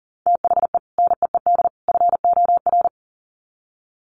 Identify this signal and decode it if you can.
Morse code, heard as a keyed tone, about 40 wpm, 700 Hz. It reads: THE NEED FOR
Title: Sets of 3 Words 40wpm